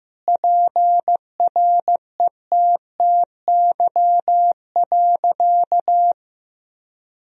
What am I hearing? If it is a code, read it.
Morse code, heard as a keyed tone, about 15 wpm, 700 Hz. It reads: PRETTY.
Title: Sentences from Top 1000 Words - Review Part 1 - 15wpm